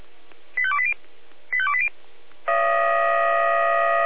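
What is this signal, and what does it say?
Five-tone selective call (ZVEI), heard as quick runs of five tones, tones of about 70 ms. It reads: 85280 85280
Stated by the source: ZVEI five-tone selective call received on SDR.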